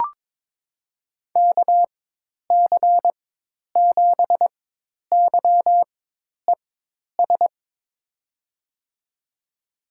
Morse code, heard as a keyed tone, about 22 wpm, spaced as for 10 wpm, 700 Hz. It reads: KC7YES
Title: USA Call Signs - 6 Characters - 4x Character Spacing - 22wpm